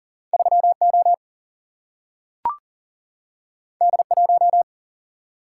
Morse code, heard as a keyed tone, about 40 wpm, 700 Hz. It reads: 3O D1
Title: Single-Letter Single-Number 40wpm